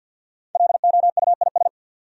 Morse code, heard as a keyed tone, about 50 wpm, 700 Hz. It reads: LOUIS
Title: USA Common Names - Max 5 Letters - 50wpm